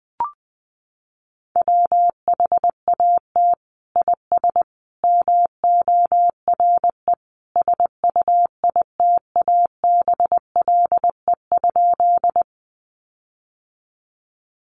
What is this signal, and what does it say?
Morse code, heard as a keyed tone, about 20 wpm, 700 Hz. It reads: WHAT IS MORE SUITABLE?
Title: Sentences from Top 1400 Words 20wpm